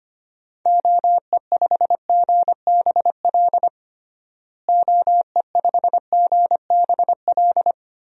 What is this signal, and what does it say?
Morse code, heard as a keyed tone, about 25 wpm, 700 Hz. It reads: OE5GBL OE5GBL
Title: DXCC-100 Call Signs - 6 Characters - Repeated 2x - 25wpm